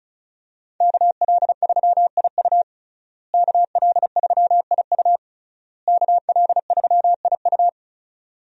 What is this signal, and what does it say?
Morse code, heard as a keyed tone, about 36 wpm, 700 Hz. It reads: KL3IU KL3IU KL3IU
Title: USA Call Signs - 5 Characters - Repeated 3x - 35wpm